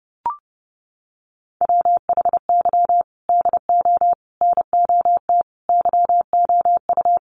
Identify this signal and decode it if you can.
Morse code, heard as a keyed tone, about 30 wpm, 700 Hz. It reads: WHY DO NOT YOU
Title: Sets of 5 Words 30wpm